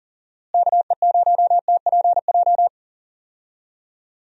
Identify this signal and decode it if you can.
Morse code, heard as a keyed tone, about 40 wpm, 700 Hz. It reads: KE0TPJ
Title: USA Call Signs - 6 Characters - Repeated 2x - 40wpm